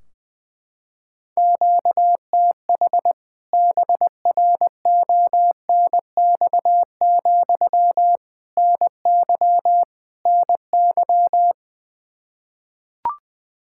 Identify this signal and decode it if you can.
Morse code, heard as a keyed tone, about 21 wpm, 700 Hz. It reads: QTH BRONX, NY NY
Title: K1USN SST QSO Practice 20wpm